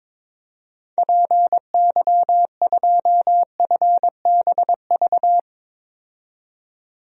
Morse code, heard as a keyed tone, about 22 wpm, 700 Hz. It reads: PY2FBV
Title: DXCC-100 Call Signs - 6 Characters - 22wpm